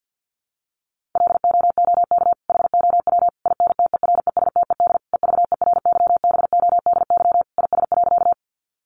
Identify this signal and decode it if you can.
Morse code, heard as a keyed tone, about 50 wpm, 700 Hz. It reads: LOOK HOW INTERESTED EVERYBODY IS.
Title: Sentences from Top 1400 Words 50wpm